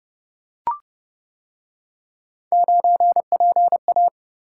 Morse code, heard as a keyed tone, about 29 wpm, 700 Hz. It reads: 9PA